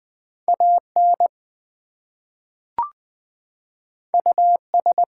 Morse code, heard as a keyed tone, about 20 wpm, 700 Hz. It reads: AN US
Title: Two Letter Words - 20wpm